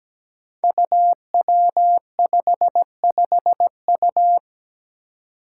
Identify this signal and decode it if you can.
Morse code, heard as a keyed tone, about 17 wpm, 700 Hz. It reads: UW55U
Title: Five Challenging Letter-Number Combinations 17wpm